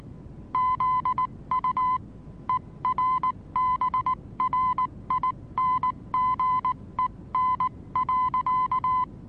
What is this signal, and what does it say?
Morse code, heard as a keyed tone, about 19 wpm, spaced as for 16 wpm, 1020 Hz. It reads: ZU ERBRINGEN.
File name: repos / Morsekurs / DJ4UF Morsen26